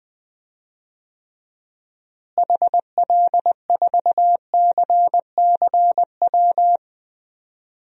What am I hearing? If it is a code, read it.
Morse code, heard as a keyed tone, about 20 wpm, 700 Hz. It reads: HL4CCW
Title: DXCC-100 Call Signs - 6 Characters - Repeated 3x - 20wpm